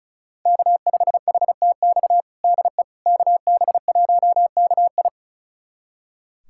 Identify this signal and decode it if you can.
Morse code, heard as a keyed tone, about 34 wpm, 690 Hz. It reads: K5HTX DE KB1KI
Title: USA Callsign DE Callsign - 5 Characters - 35wpm